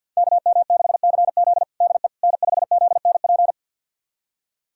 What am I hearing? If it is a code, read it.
Morse code, heard as a keyed tone, about 52 wpm, 700 Hz. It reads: KM6CZ DE N5ZNP
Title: USA Callsign DE Callsign - 5 Characters - 50wpm